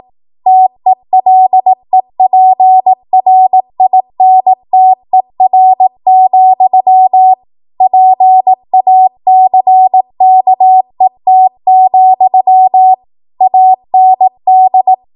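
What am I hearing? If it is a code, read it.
Morse code, heard as a keyed tone, about 18 wpm, 750 Hz. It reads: TELEPRINTER, PACKET, AND
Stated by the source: ARRL 18 wpm Morse code practice archive transmission.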